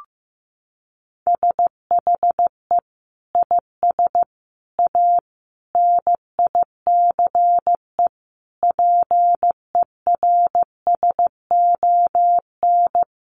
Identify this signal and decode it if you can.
Morse code, heard as a keyed tone, about 15 wpm, 700 Hz. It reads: SHE IS A NICE PERSON
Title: Sentences from Top 1100 Words 15wpm